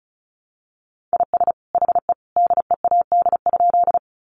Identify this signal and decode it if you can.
Morse code, heard as a keyed tone, about 34 wpm, 700 Hz. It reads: IS HE DEAD?
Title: Sentences from Top 1500 Words 35wpm